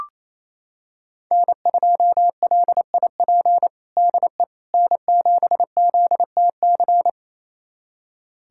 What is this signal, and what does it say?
Morse code, heard as a keyed tone, about 28 wpm, 700 Hz. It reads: N2LIP DE N7ZTC
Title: USA Callsign DE Callsign - 5 Characters - 28wpm